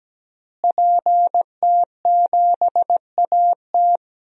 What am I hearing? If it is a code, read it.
Morse code, heard as a keyed tone, about 17 wpm, 700 Hz. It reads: PT7AT
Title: DXCC-100 Call Signs - 5 Characters - Repeated 2x - 17wpm